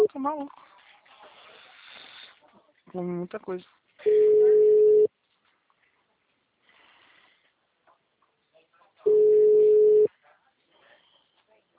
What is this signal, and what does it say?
Ringback tone (Europe, 425 Hz), with speech.